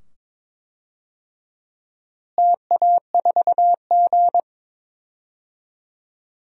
Morse code, heard as a keyed tone, about 22 wpm, 700 Hz. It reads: TA4G
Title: DXCC-100 Call Signs - 4 Characters - Repeated 2x - 22wpm